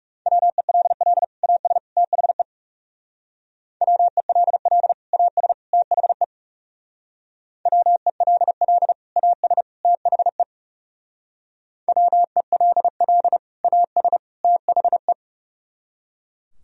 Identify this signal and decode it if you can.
Morse code, heard as a keyed tone, about 36 wpm, 700 Hz. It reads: WELL AS THE WELL AS THE WELL AS THE WELL AS THE
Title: Speed-Racing - Sets of 3 Words 30wpm